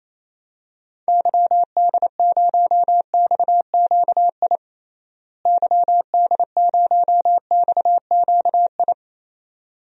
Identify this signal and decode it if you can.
Morse code, heard as a keyed tone, about 28 wpm, 700 Hz. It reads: YD0XQI YD0XQI
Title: DXCC-100 Call Signs - 6 Characters - Repeated 2x - 28wpm